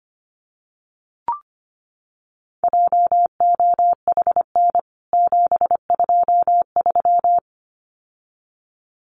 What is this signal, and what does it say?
Morse code, heard as a keyed tone, about 25 wpm, 700 Hz. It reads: JOHN 723
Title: CWT Exchange 25wpm